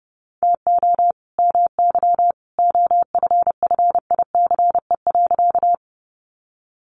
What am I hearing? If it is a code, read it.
Morse code, heard as a keyed tone, about 30 wpm, 700 Hz. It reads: TO MY OFFICE.
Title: Sentences from Top 1000 Words 30wpm